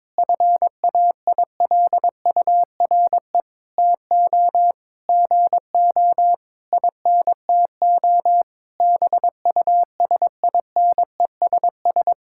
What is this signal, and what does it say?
Morse code, heard as a keyed tone, about 22 wpm, 700 Hz. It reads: FAILURE TO GO INTO BUSINESS
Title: Sentences from Top 1000 Words - Review Part 6 - 22wpm